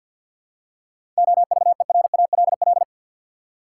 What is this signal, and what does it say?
Morse code, heard as a keyed tone, about 53 wpm, 700 Hz. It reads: OVERALL
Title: Seven Letter Words - 50wpm